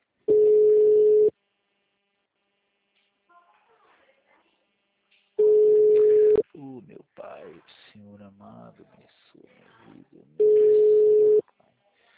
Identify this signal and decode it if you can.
Ringback tone (Europe, 425 Hz), with speech.